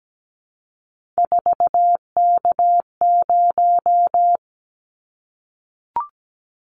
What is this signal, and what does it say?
Morse code, heard as a keyed tone, about 17 wpm, 700 Hz. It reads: 4K0
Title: Callsign Prefixes for Top 100 DXCC - 17wpm